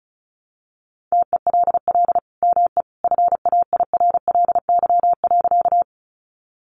Morse code, heard as a keyed tone, about 34 wpm, 700 Hz. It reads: TELL ME FAIRLY.
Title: Sentences from Top 1100 Words 35wpm